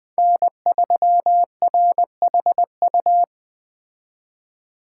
Morse code, heard as a keyed tone, about 20 wpm, 700 Hz. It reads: N3RHU